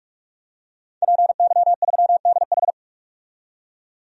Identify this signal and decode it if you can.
Morse code, heard as a keyed tone, about 46 wpm, 700 Hz. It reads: PY3DH
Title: DXCC-100 Call Signs - 5 Characters - 45wpm